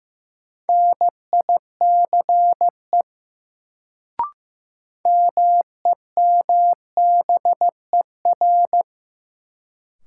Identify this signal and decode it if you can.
Morse code, heard as a keyed tone, about 15 wpm, 700 Hz. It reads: NICE MEMBER
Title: Sentences from Top 1100 Words - New Word Drill Alt - 15wpm